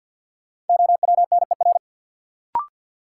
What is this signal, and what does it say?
Morse code, heard as a keyed tone, about 53 wpm, 700 Hz. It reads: OWNER
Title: Sentences from Top 1600 Words - New Word Drill - 50wpm